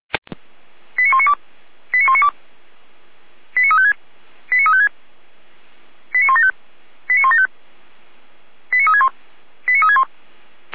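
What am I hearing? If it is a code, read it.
Five-tone selective call (ZVEI), heard as quick runs of five tones, tones of about 70 ms. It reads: 89192 89192 89356 89356 89175 89175 89361 89361
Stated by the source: ZVEI five-tone call received by SDR, recorded by the Orlamünde fire brigade (feuerwehr-orlamuende).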